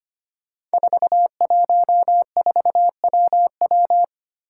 Morse code, heard as a keyed tone, about 25 wpm, 700 Hz. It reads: 414WW
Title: Five Challenging Letter-Number Combinations 25wpm